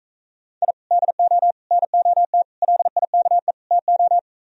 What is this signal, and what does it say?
Morse code, heard as a keyed tone, about 45 wpm, 700 Hz. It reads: I DO NOT LIKE TO